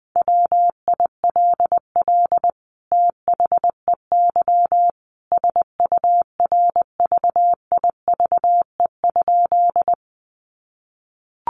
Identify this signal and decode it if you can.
Morse code, heard as a keyed tone, about 20 wpm, 700 Hz. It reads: WILL THEY SURVIVE?
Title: Sentences from Top 1300 Words 20wpm